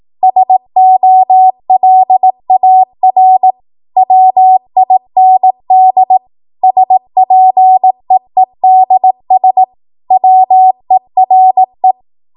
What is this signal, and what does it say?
Morse code, heard as a keyed tone, about 18 wpm, 750 Hz. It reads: SOLAR WIND SPEEDS WERE
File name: Morse / 18WPM